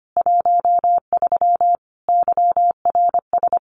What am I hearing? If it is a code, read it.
Morse code, heard as a keyed tone, about 25 wpm, 700 Hz. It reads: 13 YRS
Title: Common QSO Phrases 25wpm